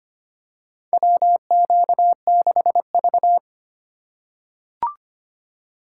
Morse code, heard as a keyed tone, about 25 wpm, 700 Hz. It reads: WQ6V
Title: USA Call Signs - 4 Characters - 4x Character Spacing - 25wpm